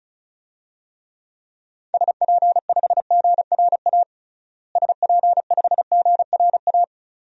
Morse code, heard as a keyed tone, about 36 wpm, 700 Hz. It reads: SP5GRA SP5GRA
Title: DXCC-100 Call Signs - 6 Characters - Repeated 2x - 35wpm